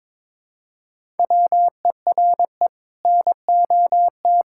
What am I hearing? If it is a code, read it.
Morse code, heard as a keyed tone, about 22 wpm, 700 Hz. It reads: WERE NOT
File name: practice / 22wpm / Sets of 7 Words - 22wpm